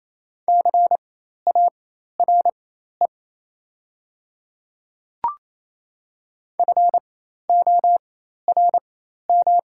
Morse code, heard as a keyed tone, about 28 wpm, spaced as for 13 wpm, 700 Hz. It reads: CARE FORM
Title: 4x Character Spacing - Top 500 Words - 28wpm